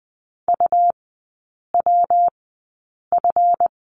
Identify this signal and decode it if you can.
Morse code, heard as a keyed tone, about 20 wpm, spaced as for 8 wpm, 700 Hz. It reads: UWF
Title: 152 Lesson 25e - Letter p - Three Characters 20wpm